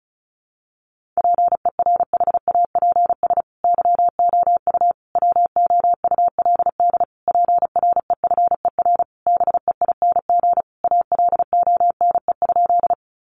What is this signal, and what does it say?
Morse code, heard as a keyed tone, about 34 wpm, 700 Hz. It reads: PERHAPS YOU WOULD PREFER BEING ALONE?
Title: Sentences from Top 1100 Words 35wpm